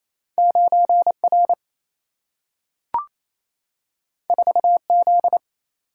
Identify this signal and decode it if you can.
Morse code, heard as a keyed tone, about 28 wpm, 700 Hz. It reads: 9 R 4 Z
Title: Single-Letter Single-Number 28wpm